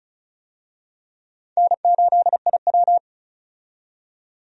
Morse code, heard as a keyed tone, about 36 wpm, 700 Hz. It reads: N8IW
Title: USA Call Signs - 4 Characters - 35wpm